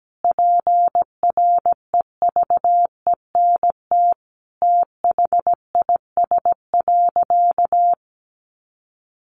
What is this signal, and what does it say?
Morse code, heard as a keyed tone, about 17 wpm, 700 Hz. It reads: PREVENT THIS.
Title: Sentences from Top 1100 Words 17wpm